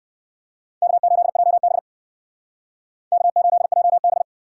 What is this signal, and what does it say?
Morse code, heard as a keyed tone, about 59 wpm, 700 Hz. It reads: K8JB K8JB